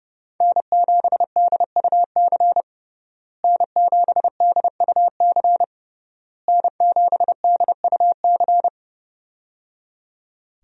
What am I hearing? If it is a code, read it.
Morse code, heard as a keyed tone, about 30 wpm, 700 Hz. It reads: N7DUC N7DUC N7DUC
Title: USA Call Signs - 5 Characters - Repeated 3x - 30wpm